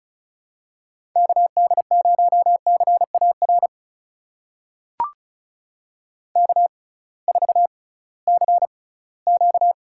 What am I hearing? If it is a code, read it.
Morse code, heard as a keyed tone, about 36 wpm, spaced as for 20 wpm, 700 Hz. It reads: KD0CAR K4CQ
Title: USA Call Signs - 6x Character Spacing - 35wpm